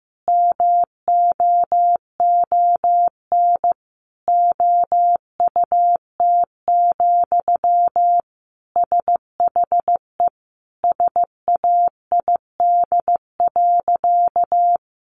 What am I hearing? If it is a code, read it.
Morse code, heard as a keyed tone, about 15 wpm, 700 Hz. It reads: MOON OUT, SHE SAID.
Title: Sentences from Top 500 Words 15wpm